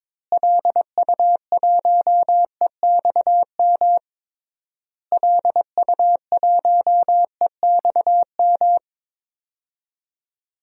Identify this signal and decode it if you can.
Morse code, heard as a keyed tone, about 21 wpm, 700 Hz. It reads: LU1EXM LU1EXM
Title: DXCC-100 Call Signs - 6 Characters - Repeated 3x - 22wpm